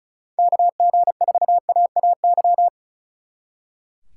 Morse code, heard as a keyed tone, about 45 wpm, 700 Hz. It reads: KG4AAY